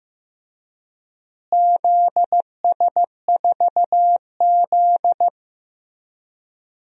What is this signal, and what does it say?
Morse code, heard as a keyed tone, about 15 wpm, 700 Hz. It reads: ZS4Z
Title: DXCC-100 Call Signs - 4 Characters - Repeated 3x - 15wpm